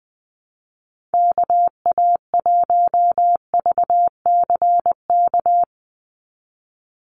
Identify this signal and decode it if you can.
Morse code, heard as a keyed tone, about 19 wpm, 700 Hz. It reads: KA1VCK
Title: USA Call Signs 20wpm